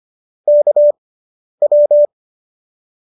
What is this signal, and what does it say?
Morse code, heard as a keyed tone, about 25 wpm, spaced as for 10 wpm, 590 Hz. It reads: KW